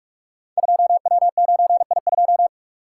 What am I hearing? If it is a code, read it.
Morse code, heard as a keyed tone, about 44 wpm, 700 Hz. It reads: 2W9I2
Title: Code Groups - 5 Letter:Numbers - 45wpm